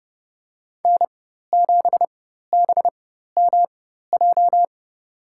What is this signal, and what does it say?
Morse code, heard as a keyed tone, about 30 wpm, spaced as for 14 wpm, 700 Hz. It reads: N7BMJ
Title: USA Call Signs - 5 Characters - 4x Character Spacing - 30wpm